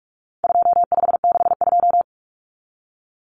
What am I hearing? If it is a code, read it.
Morse code, heard as a keyed tone, about 44 wpm, 700 Hz. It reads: 2562
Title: Common Cut Numbers - 4-Digit 45wpm